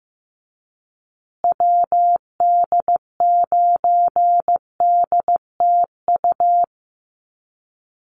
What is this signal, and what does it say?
Morse code, heard as a keyed tone, about 15 wpm, 700 Hz. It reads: WD9DTU
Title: USA Call Signs Repeated 3x 15wpm